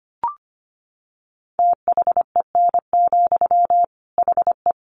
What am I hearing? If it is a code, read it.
Morse code, heard as a keyed tone, about 25 wpm, 700 Hz. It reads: THEN, HE
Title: Sentences from Top 1500 Words 25wpm